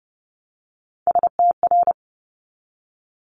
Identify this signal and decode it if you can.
Morse code, heard as a keyed tone, about 30 wpm, 700 Hz. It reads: STR
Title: Multi-letter Phonemes 30wpm